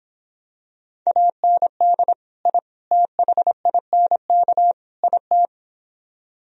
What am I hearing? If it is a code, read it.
Morse code, heard as a keyed tone, about 25 wpm, 700 Hz. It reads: AND I THINK IT
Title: Speed-Racing - Sets of 4 Words 22wpm